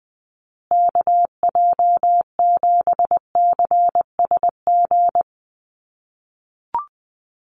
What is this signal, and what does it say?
Morse code, heard as a keyed tone, about 20 wpm, 700 Hz. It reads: KJ7CSG
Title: USA Call Signs Repeated 2x 20wpm